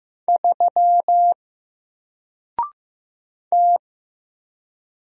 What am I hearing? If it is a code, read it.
Morse code, heard as a keyed tone, about 15 wpm, 700 Hz. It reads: 3 T
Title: Single Letter-Number Alt 15wpm